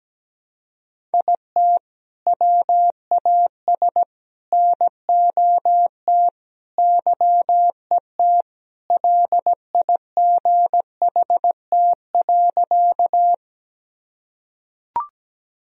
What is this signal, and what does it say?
Morse code, heard as a keyed tone, about 17 wpm, 700 Hz. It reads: IT WAS NOT YET LIGHT.
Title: Sentences from Top 1000 Words - Review Part 1 - 17wpm